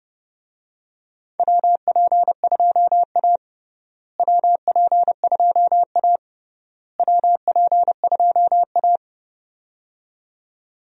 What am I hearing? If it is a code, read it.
Morse code, heard as a keyed tone, about 30 wpm, 700 Hz. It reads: WP2A WP2A WP2A
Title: USA Call Signs - 4 Characters - Repeated 3x - 30wpm